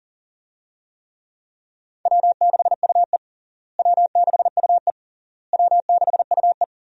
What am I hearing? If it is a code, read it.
Morse code, heard as a keyed tone, about 37 wpm, 700 Hz. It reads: W6UE W6UE W6UE
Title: USA Call Signs - 4 Characters - Repeated 3x - 40wpm